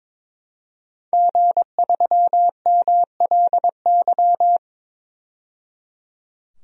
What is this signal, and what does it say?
Morse code, heard as a keyed tone, about 22 wpm, 700 Hz. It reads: G3MLY
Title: DXCC-100 Call Signs - 5 Characters - 22wpm